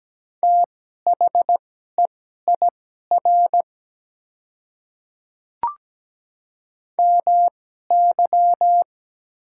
Morse code, heard as a keyed tone, about 17 wpm, spaced as for 12 wpm, 700 Hz. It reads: THEIR MY
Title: 2x Character Spacing - Top 100 Words - 17wpm